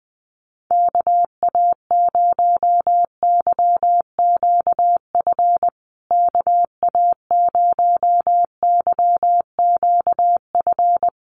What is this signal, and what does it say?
Morse code, heard as a keyed tone, about 20 wpm, 700 Hz. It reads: KA0YQF KA0YQF
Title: USA Call Signs Repeated 2x 20wpm